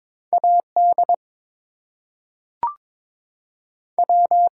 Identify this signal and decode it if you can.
Morse code, heard as a keyed tone, about 22 wpm, 700 Hz. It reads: A D W
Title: Two Letter Words - 22wpm